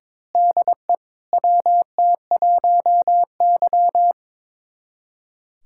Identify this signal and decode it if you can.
Morse code, heard as a keyed tone, about 21 wpm, 700 Hz. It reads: DE WT1Y
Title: USA Callsign DE Callsign 22wpm